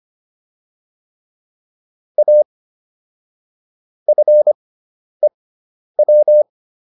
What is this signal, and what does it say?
Morse code, heard as a keyed tone, about 25 wpm, spaced as for 10 wpm, 600 Hz. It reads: A FEW